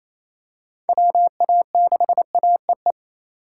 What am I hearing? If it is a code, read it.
Morse code, heard as a keyed tone, about 28 wpm, 700 Hz. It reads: WA6AEE
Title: K1USN SST QSO Practice - 28wpm